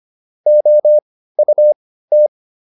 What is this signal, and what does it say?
Morse code, heard as a keyed tone, about 25 wpm, spaced as for 15 wpm, 600 Hz. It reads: OUT